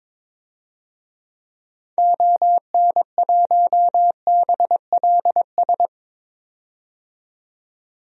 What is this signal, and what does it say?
Morse code, heard as a keyed tone, about 22 wpm, 700 Hz. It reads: ON1BLS
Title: DXCC-100 Call Signs - 6 Characters - 22wpm